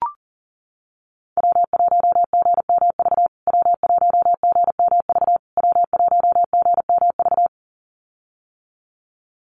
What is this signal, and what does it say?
Morse code, heard as a keyed tone, about 40 wpm, 700 Hz. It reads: W1GMV W1GMV W1GMV
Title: USA Call Signs Repeated 3x 40wpm